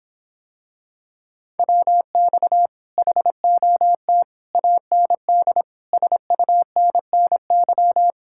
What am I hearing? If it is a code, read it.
Morse code, heard as a keyed tone, about 26 wpm, 700 Hz. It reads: WX HOT AND SUNNY